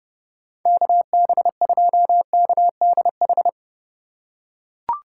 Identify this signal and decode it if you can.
Morse code, heard as a keyed tone, about 31 wpm, 700 Hz. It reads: KB2KDH